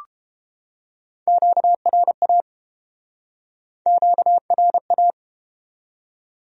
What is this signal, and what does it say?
Morse code, heard as a keyed tone, about 31 wpm, 700 Hz. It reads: QRA QRA